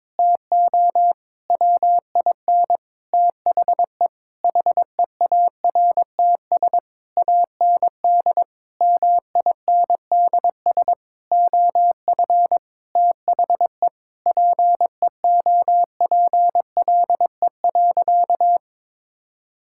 Morse code, heard as a keyed tone, about 22 wpm, 700 Hz. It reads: TO WIN THE HEARTS AND MINDS OF THE PEOPLE.
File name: practice / 22wpm / Sentences with 8-Word Phrases - 22wpm